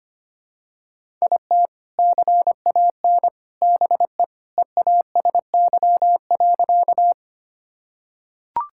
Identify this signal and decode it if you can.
Morse code, heard as a keyed tone, about 25 wpm, 700 Hz. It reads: IT CAN BE EASY.
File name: practice / 25wpm / Sentences with 3-Word Phrases - 25wpm